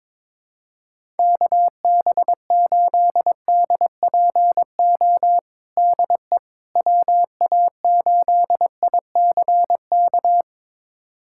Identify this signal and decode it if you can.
Morse code, heard as a keyed tone, about 22 wpm, 700 Hz. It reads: KB8DPO DE WA8ICK
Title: USA Callsign DE Callsign - 6 Characters - 22wpm